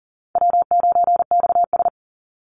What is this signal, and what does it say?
Morse code, heard as a keyed tone, about 40 wpm, 700 Hz. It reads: W9XS
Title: Easy Call Signs - 40wpm